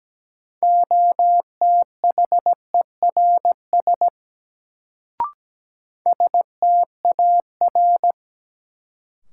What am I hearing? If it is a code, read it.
Morse code, heard as a keyed tone, about 17 wpm, 700 Hz. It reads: OTHERS STAR
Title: Sentences from Top 1000 Words - Review Part 2 - New Word Drill - 17wpm